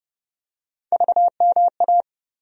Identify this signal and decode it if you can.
Morse code, heard as a keyed tone, about 31 wpm, 700 Hz. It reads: VMA
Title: USA Call Sign Postfixes - 3 Letters 30wpm